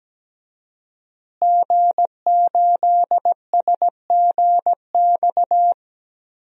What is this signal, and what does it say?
Morse code, heard as a keyed tone, about 17 wpm, 700 Hz. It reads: G8SGX
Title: DXCC-100 Call Signs - 5 Characters - Repeated 3x - 17wpm